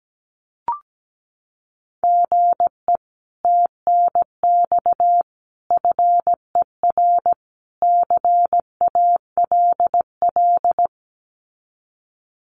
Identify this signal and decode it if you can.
Morse code, heard as a keyed tone, about 17 wpm, 700 Hz. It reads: GE TNX FER CALL
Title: Common QSO Phrases Encore 17wpm